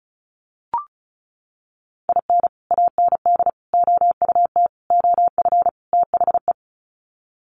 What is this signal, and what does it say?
Morse code, heard as a keyed tone, about 34 wpm, 700 Hz. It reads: IN AND OUT OF THE
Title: Sets of 5 Words 35wpm